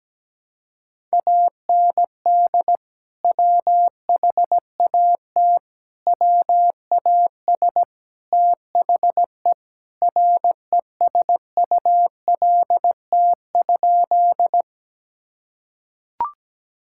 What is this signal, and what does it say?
Morse code, heard as a keyed tone, about 17 wpm, 700 Hz. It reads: AND WHAT WAS THE RESULT?
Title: Sentences from Top 1000 Words - Review Part 5 - 17wpm